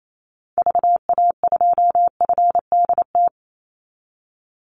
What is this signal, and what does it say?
Morse code, heard as a keyed tone, about 28 wpm, 700 Hz. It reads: VA2FDT
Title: Call Signs - 28wpm